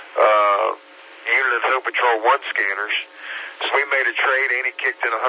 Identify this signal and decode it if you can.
Emergency Alert System (SAME) tones, heard as harsh buzzing bursts, with speech. It is heard from 1.2 to 4.1 s.